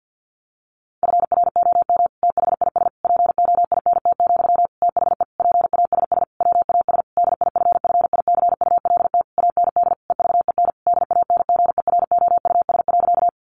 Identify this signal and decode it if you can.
Morse code, heard as a keyed tone, about 50 wpm, 700 Hz. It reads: FROM THIS POINT, THE PASS WAS DIFFICULT AND EVEN DANGEROUS.
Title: Sentences from Top 1200 Words 50wpm